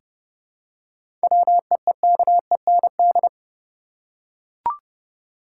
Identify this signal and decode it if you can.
Morse code, heard as a keyed tone, about 30 wpm, 700 Hz. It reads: WEEKEND
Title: Seven Letter Words - 30wpm